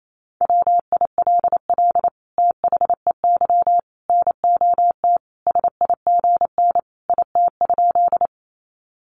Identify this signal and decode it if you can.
Morse code, heard as a keyed tone, about 28 wpm, 700 Hz. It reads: WILL THEY NOT SIGN IT?
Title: Sentences from Top 1000 Words 28wpm